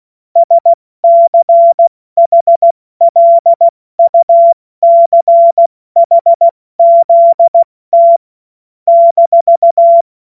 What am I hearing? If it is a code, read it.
Morse code, heard as a keyed tone, about 16 wpm, spaced as for 14 wpm, 680 Hz. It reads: SCHLUCHZT -
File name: Witz-16-September-2025-70-F80